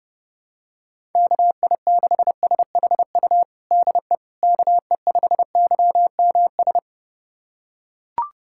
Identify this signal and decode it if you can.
Morse code, heard as a keyed tone, about 30 wpm, 700 Hz. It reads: KI6SHU DE KE5YMS
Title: USA Callsign DE Callsign - 6 Characters - 30wpm